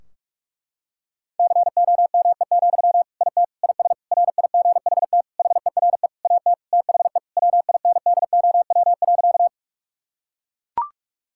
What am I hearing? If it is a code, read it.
Morse code, heard as a keyed tone, about 41 wpm, 700 Hz. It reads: COME, IT IS RIGHT HERE AT THE WINDOW.
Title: Sentences from Top 1600 Words 45wpm